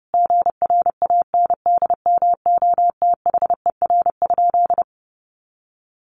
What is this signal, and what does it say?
Morse code, heard as a keyed tone, about 30 wpm, 700 Hz. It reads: GRANDMOTHER?
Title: Sentences from Top 1200 Words 30wpm